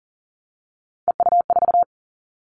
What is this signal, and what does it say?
Morse code, heard as a keyed tone, about 40 wpm, 700 Hz. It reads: EU4
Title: Callsign Prefixes for Top 50 DXCC - 40wpm